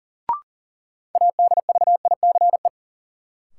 Morse code, heard as a keyed tone, about 40 wpm, 700 Hz. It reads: ADVICE
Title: Sentences from Top 1100 Words - New Word Drill Alt - 40wpm